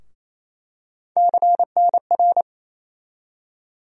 Morse code, heard as a keyed tone, about 28 wpm, 700 Hz. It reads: CNR
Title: USA Call Sign Postfixes - 3 Letters 28wpm